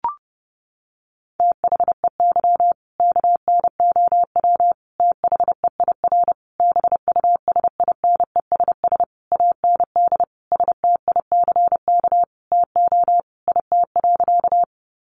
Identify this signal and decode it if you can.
Morse code, heard as a keyed tone, about 28 wpm, 700 Hz. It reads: THEY KNOW THEIR BUSINESS AND STICK TO IT.
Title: Sentences from Top 1200 Words 30wpm